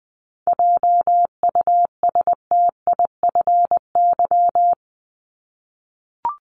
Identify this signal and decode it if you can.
Morse code, heard as a keyed tone, about 19 wpm, 700 Hz. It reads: JUSTIFY
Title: Top 1751-2000 Words 20wpm